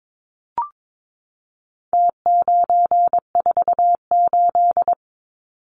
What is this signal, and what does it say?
Morse code, heard as a keyed tone, about 22 wpm, 700 Hz. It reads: T948
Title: Numbers - Contest Style - 4-Digit 22wpm